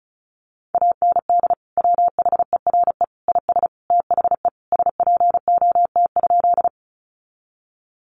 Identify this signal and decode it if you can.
Morse code, heard as a keyed tone, about 34 wpm, 700 Hz. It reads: AND WHERE IS THE SPOT?
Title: Sentences from Top 1100 Words 35wpm